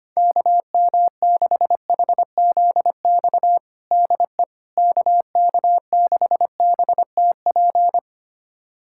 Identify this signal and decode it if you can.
Morse code, heard as a keyed tone, about 25 wpm, 700 Hz. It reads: KM6HZX DE KK6BTP
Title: USA Callsign DE Callsign 25wpm